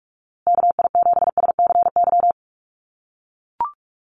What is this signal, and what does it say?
Morse code, heard as a keyed tone, about 44 wpm, 700 Hz. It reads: KI7SCY
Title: USA Call Signs Repeated 3x 45wpm